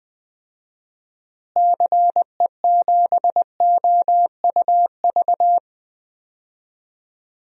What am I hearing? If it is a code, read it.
Morse code, heard as a keyed tone, about 21 wpm, 700 Hz. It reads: CE7OUV
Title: DXCC-100 Call Signs - 6 Characters - Repeated 3x - 20wpm